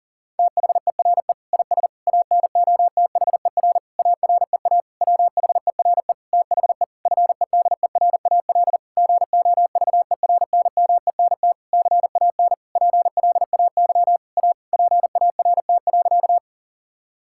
Morse code, heard as a keyed tone, about 40 wpm, 700 Hz. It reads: THERE IS ANOTHER AREA WHERE THE FEDERAL GOVERNMENT CAN PLAY A PART.
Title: Sentences from Top 1000 Words - Review Part 5 - 40wpm